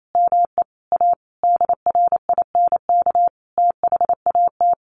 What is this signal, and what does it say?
Morse code, heard as a keyed tone, about 28 wpm, 700 Hz. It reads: ME A DRINK THAT
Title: Sentences from Top 1500 Words 28wpm